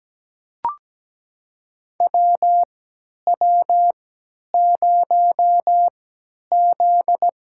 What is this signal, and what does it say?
Morse code, heard as a keyed tone, about 17 wpm, spaced as for 10 wpm, 700 Hz. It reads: WW0Z
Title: USA Call Signs - 3x Character Spacing - 17wpm